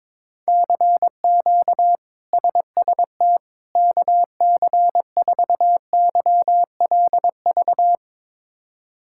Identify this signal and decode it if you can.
Morse code, heard as a keyed tone, about 22 wpm, 700 Hz. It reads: CQ SST KC4YLV
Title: K1USN SST QSO Practice - 22wpm